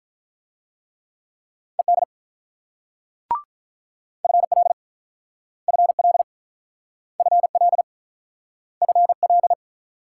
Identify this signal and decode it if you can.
Morse code, heard as a keyed tone, about 44 wpm, 700 Hz. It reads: ED FL FL FL FL